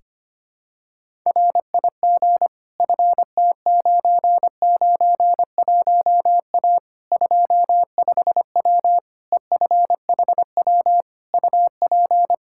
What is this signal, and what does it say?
Morse code, heard as a keyed tone, about 25 wpm, 700 Hz. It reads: RIG FT991A 25W EFHW UP
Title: Practice QSO - Exchange 2 - Info - 25wpm